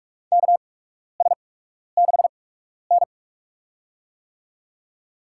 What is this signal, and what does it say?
Morse code, heard as a keyed tone, about 46 wpm, spaced as for 12 wpm, 690 Hz. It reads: KS6N
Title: USA Call Signs - 4 Characters - 8x Character Spacing - 45wpm